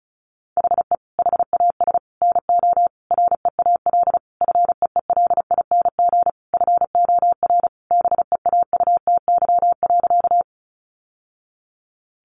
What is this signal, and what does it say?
Morse code, heard as a keyed tone, about 34 wpm, 700 Hz. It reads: HE HAS NO REAL FEELING FOR BEAUTY.
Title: Sentences from Top 1200 Words 35wpm